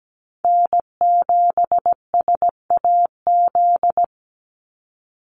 Morse code, heard as a keyed tone, about 17 wpm, 700 Hz. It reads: N7SAZ
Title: USA Call Signs 17wpm